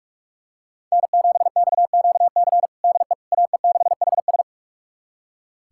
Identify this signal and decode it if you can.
Morse code, heard as a keyed tone, about 42 wpm, 700 Hz. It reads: N7XQC DE AE6HS
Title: USA Callsign DE Callsign - 5 Characters - 45wpm